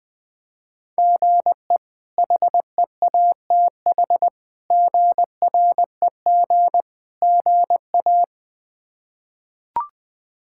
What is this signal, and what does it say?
Morse code, heard as a keyed tone, about 20 wpm, 700 Hz. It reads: GE HEATH GREG GA
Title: K1USN SST QSO Practice - 20wpm